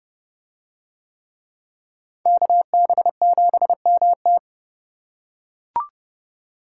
Morse code, heard as a keyed tone, about 29 wpm, 700 Hz. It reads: KB7MT